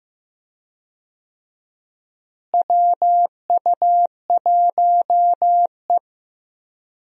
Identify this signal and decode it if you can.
Morse code, heard as a keyed tone, about 15 wpm, 700 Hz. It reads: WU1E
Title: USA Call Signs - 4 Characters - Repeated 3x - 15wpm